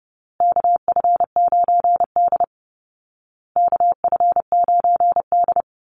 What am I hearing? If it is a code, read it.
Morse code, heard as a keyed tone, about 30 wpm, 700 Hz. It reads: KF9D KF9D
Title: Call Signs Repeated 2x - 30wpm